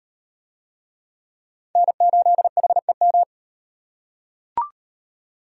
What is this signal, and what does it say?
Morse code, heard as a keyed tone, about 38 wpm, 700 Hz. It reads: N8HEM